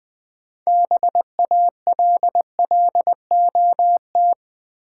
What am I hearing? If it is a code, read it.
Morse code, heard as a keyed tone, about 20 wpm, 700 Hz. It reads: BALLOT
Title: Build A Word - 3 Letters At A Time - 6-Letter Words - 20wpm